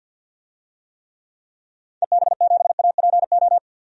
Morse code, heard as a keyed tone, about 50 wpm, 700 Hz. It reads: EB7APO
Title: DXCC-100 Call Signs - 6 Characters - 50wpm